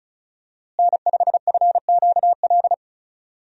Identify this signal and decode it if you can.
Morse code, heard as a keyed tone, about 34 wpm, 700 Hz. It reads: N5FQL
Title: USA Call Signs - 5 Characters - Repeated 2x - 35wpm